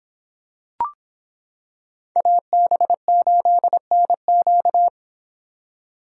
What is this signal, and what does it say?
Morse code, heard as a keyed tone, about 25 wpm, 700 Hz. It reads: AB8NQ